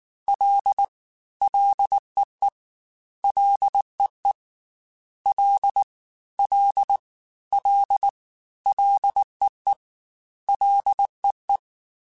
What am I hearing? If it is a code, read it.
Morse code, heard as a keyed tone, about 19 wpm, 800 Hz. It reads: L LEE LEE L L L LEE LEE
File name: F and L / F and L mp3 / F and L Prefixes 3x2 Twice @18